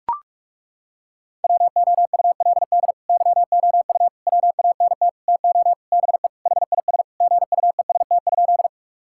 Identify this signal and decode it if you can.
Morse code, heard as a keyed tone, about 41 wpm, 700 Hz. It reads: WOULD YOU WANT TO BE HIS GUEST?
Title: Sentences from Top 1600 Words 45wpm